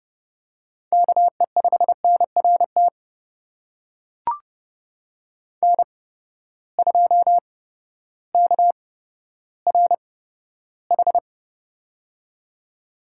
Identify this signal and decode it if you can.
Morse code, heard as a keyed tone, about 30 wpm, spaced as for 14 wpm, 700 Hz. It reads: KE5NRT N2KRH
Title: USA Call Signs - 8x Character Spacing - 30wpm